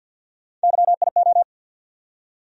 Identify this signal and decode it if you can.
Morse code, heard as a keyed tone, about 47 wpm, 700 Hz. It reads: YIO
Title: USA Call Sign Postfixes - 3 Letters 50wpm